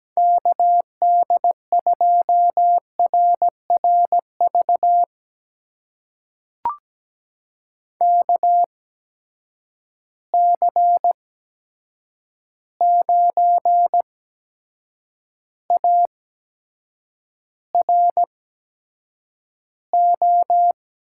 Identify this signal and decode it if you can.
Morse code, heard as a keyed tone, about 17 wpm, spaced as for 7 wpm, 700 Hz. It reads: KD2RRV KC9ARO
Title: USA Call Signs - 6 Characters - 8x Character Spacing - 17wpm